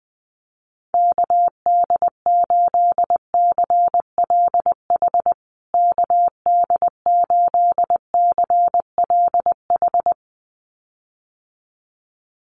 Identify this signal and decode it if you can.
Morse code, heard as a keyed tone, about 20 wpm, 700 Hz. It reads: KD8CLH KD8CLH
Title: USA Call Signs Repeated 2x 20wpm